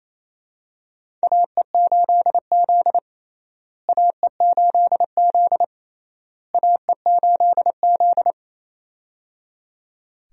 Morse code, heard as a keyed tone, about 28 wpm, 700 Hz. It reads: AE8Z AE8Z AE8Z
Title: USA Call Signs - 4 Characters - Repeated 3x - 28wpm